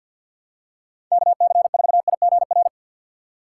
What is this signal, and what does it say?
Morse code, heard as a keyed tone, about 52 wpm, 700 Hz. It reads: KC4IGR
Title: USA Call Signs - 3x Character Spacing - 50wpm